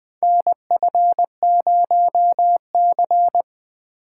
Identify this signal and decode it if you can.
Morse code, heard as a keyed tone, about 20 wpm, 700 Hz. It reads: NF0C
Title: USA Call Signs - 4 Characters - 2x Character Spacing - 20wpm